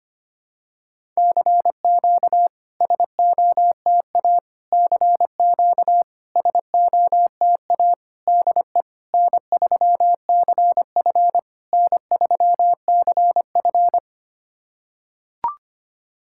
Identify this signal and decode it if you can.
Morse code, heard as a keyed tone, about 25 wpm, 700 Hz. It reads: CQ SOTA CQ SOTA DE N3CF N3CF
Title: SOTA QSO Practice 25wpm